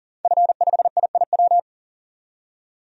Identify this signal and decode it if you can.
Morse code, heard as a keyed tone, about 40 wpm, 700 Hz. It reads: F5IIW
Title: DXCC-100 Call Signs - 5 Characters - 40wpm